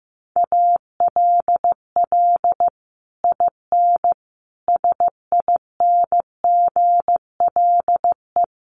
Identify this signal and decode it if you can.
Morse code, heard as a keyed tone, about 15 wpm, 700 Hz. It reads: ALL IN SINGLE
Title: Sentences from Top 1200 Words 15wpm